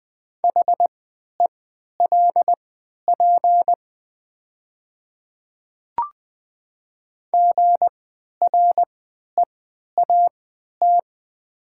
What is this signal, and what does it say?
Morse code, heard as a keyed tone, about 20 wpm, spaced as for 11 wpm, 700 Hz. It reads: HELP GREAT
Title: 3x Character Spacing - Top 200 Words - 20wpm